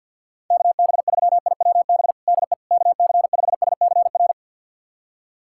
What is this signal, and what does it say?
Morse code, heard as a keyed tone, about 51 wpm, 690 Hz. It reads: KB3IWB DE KC5SCR
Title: USA Callsign DE Callsign - 6 Characters - 50wpm